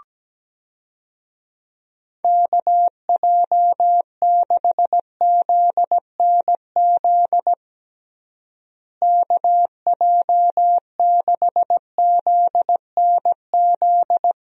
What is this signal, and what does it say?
Morse code, heard as a keyed tone, about 17 wpm, 700 Hz. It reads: KJ6ZNZ KJ6ZNZ
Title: USA Call Signs - 6 Characters - Repeated 3x - 17wpm